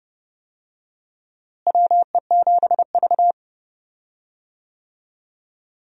Morse code, heard as a keyed tone, about 30 wpm, 700 Hz. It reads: WE7V
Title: USA Call Signs - 4 Characters - Repeated 2x - 30wpm